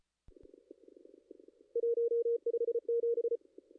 Morse code, heard as a keyed tone, about 34 wpm, 470 Hz. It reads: 157
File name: contests / ss-cw-k5tr-op-n5rz -2300